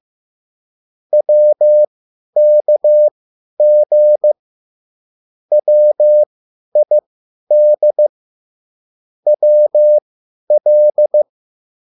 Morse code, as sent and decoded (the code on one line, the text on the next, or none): .-- -.- --. / .-- .. -.. / .-- .-..
WKG WID WL